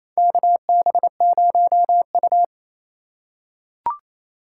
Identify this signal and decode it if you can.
Morse code, heard as a keyed tone, about 28 wpm, 700 Hz. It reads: KB0U
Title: USA Call Signs - 4 Characters - 2x Character Spacing - 28wpm